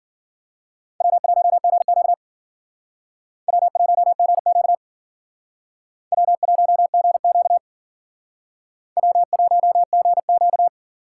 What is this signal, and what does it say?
Morse code, heard as a keyed tone, about 52 wpm, 700 Hz. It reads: W1GQ W1GQ W1GQ W1GQ